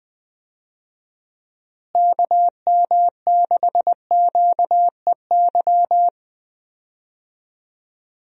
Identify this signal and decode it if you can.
Morse code, heard as a keyed tone, about 20 wpm, 700 Hz. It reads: KM6QEY
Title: USA Call Signs - 6 Characters - Repeated 2x - 20wpm